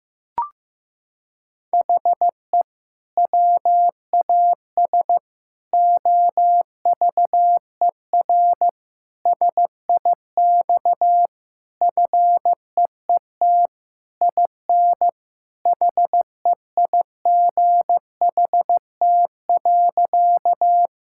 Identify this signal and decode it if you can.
Morse code, heard as a keyed tone, about 16 wpm, 700 Hz. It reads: HE WAS OVER SIX FEET IN HEIGHT.
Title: Sentences from Top 1600 Words 15wpm